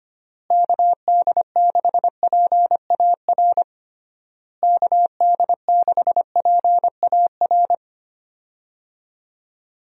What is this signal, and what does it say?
Morse code, heard as a keyed tone, about 25 wpm, 700 Hz. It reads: KD6PAR KD6PAR
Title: USA Call Signs - 6 Characters - Repeated 3x - 25wpm